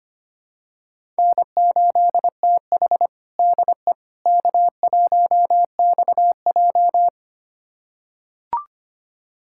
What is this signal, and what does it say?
Morse code, heard as a keyed tone, about 25 wpm, 700 Hz. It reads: N8TH DE K1XJ
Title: USA Callsign DE Callsign - 4 Characters - 25wpm